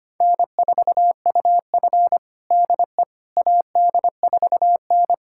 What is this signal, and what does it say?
Morse code, heard as a keyed tone, about 25 wpm, 700 Hz. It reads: N4UF DE AD4N
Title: USA Callsign DE Callsign - 4 Characters - 25wpm